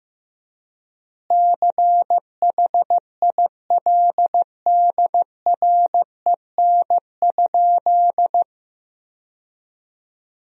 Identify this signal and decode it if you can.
Morse code, heard as a keyed tone, about 15 wpm, 700 Hz. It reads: CHILDREN?
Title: 3x Word-Spacing - Sentences from Top 300 Words - 15wpm